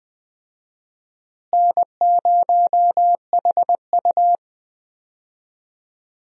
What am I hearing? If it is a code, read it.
Morse code, heard as a keyed tone, about 20 wpm, 700 Hz. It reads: N0HU
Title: USA Call Signs - 4 Characters - Repeated 2x - 20wpm